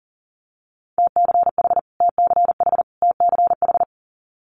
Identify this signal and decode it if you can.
Morse code, heard as a keyed tone, about 40 wpm, 700 Hz. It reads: TCH TCH TCH
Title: Multi-letter Phonemes Repeated 3x 40wpm